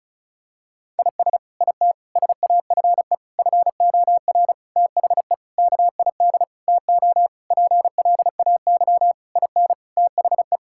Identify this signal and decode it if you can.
Morse code, heard as a keyed tone, about 36 wpm, 700 Hz. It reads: IS IT SAFE FOR THE KID TO PLAY IN THE
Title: Sentences from Top 1600 Words 35wpm